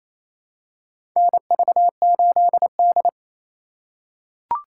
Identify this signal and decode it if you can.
Morse code, heard as a keyed tone, about 28 wpm, 700 Hz. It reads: NV8D
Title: USA Call Signs - 4 Characters - 3x Character Spacing - 28wpm